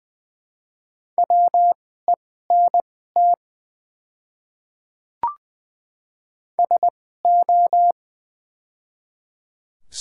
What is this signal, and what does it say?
Morse code, heard as a keyed tone, about 20 wpm, spaced as for 14 wpm, 700 Hz. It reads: WENT SO
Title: 2x Character Spacing - Top 200 Words - 20wpm